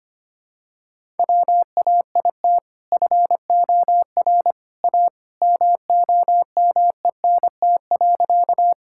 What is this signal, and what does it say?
Morse code, heard as a keyed tone, about 25 wpm, 700 Hz. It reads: WAIT FOR A MOMENT.
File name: practice / 25wpm / Sentences with 2-Word Phrases - 25wpm